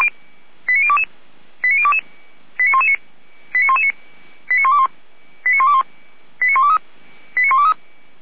Five-tone selective call (ZVEI), heard as quick runs of five tones, tones of about 80 ms. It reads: 89022 89022 89119 89119 89121 89121 89123 89123